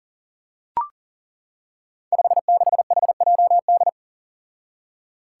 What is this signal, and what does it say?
Morse code, heard as a keyed tone, about 39 wpm, 700 Hz. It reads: 56HJD
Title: Five Challenging Letter-Number Combinations 40wpm